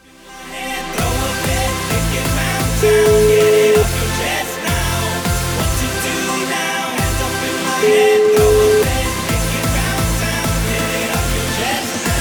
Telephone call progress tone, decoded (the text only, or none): ringback tone